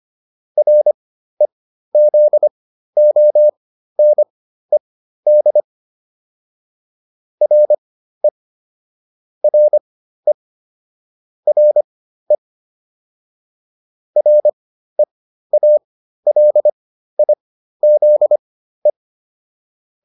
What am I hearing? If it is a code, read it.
Morse code, heard as a keyed tone, about 25 wpm, spaced as for 13 wpm, 600 Hz. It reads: REZONED RE RE RE REALIZE